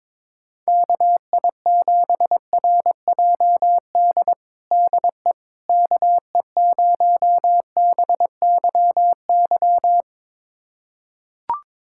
Morse code, heard as a keyed tone, about 22 wpm, 700 Hz. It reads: KI7RJD DE KE0BYY
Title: USA Callsign DE Callsign - 6 Characters - 22wpm